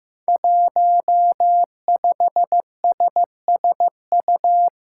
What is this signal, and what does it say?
Morse code, heard as a keyed tone, about 15 wpm, 700 Hz. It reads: 15SSU
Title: Five Challenging Letter-Number Combinations 15wpm